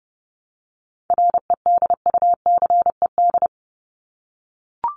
Morse code, heard as a keyed tone, about 30 wpm, 700 Hz. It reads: REDUCED
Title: ED Suffix - Top 100 Words with Base Words 30wpm